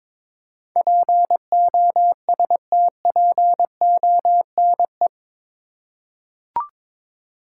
Morse code, heard as a keyed tone, about 22 wpm, 700 Hz. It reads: POSTPONE
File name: practice / 22wpm / Eight Letter Words - 22wpm